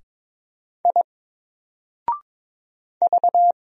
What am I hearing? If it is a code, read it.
Morse code, heard as a keyed tone, about 22 wpm, 700 Hz. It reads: I V
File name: Single-Letter-Number-22wpm